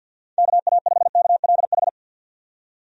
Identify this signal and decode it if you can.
Morse code, heard as a keyed tone, about 51 wpm, 700 Hz. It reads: KA5KLH
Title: USA Call Signs - 6 Characters - 50wpm